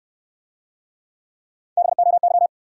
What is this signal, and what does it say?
Morse code, heard as a keyed tone, about 69 wpm, 700 Hz. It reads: BOY